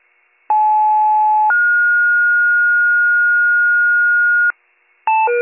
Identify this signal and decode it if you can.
Two-tone page, heard as a one-second tone, then a three-second tone, about 860 then 1430 Hz, with noise. It is heard from 0.5 to 4.5 s.